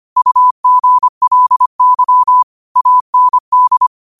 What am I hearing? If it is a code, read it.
Morse code, heard as a keyed tone, about 25 wpm, 1000 Hz. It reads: UGLY AND